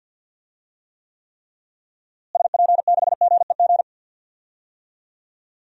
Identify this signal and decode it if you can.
Morse code, heard as a keyed tone, about 46 wpm, 700 Hz. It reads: SP6GEG
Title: DXCC-100 Call Signs - 6 Characters - Repeated 3x - 50wpm